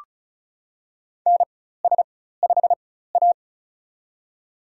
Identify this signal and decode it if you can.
Morse code, heard as a keyed tone, about 36 wpm, spaced as for 16 wpm, 700 Hz. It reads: NS5A
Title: USA Call Signs - 4x Character Spacing - 35wpm